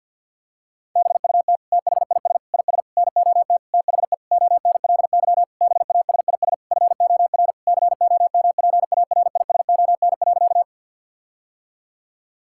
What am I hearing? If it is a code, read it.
Morse code, heard as a keyed tone, about 50 wpm, 700 Hz. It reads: BUT THIS IS NOT THE ONLY BASIS FOR COMPARISON.
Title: Sentences from Top 1600 Words 50wpm